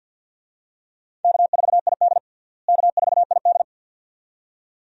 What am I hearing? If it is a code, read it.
Morse code, heard as a keyed tone, about 53 wpm, 700 Hz. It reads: K4ID K4ID
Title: USA Call Signs - 4 Characters - Repeated 2x - 50wpm